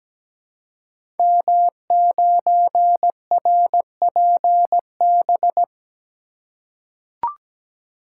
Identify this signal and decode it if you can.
Morse code, heard as a keyed tone, about 17 wpm, 700 Hz. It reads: M9RPB
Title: Code Groups - 5 Letter:Numbers - 17wpm